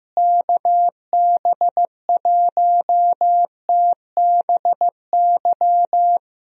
Morse code, heard as a keyed tone, about 15 wpm, 700 Hz. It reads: KB1TBY
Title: USA Call Signs - 6 Characters - Repeated 3x - 15wpm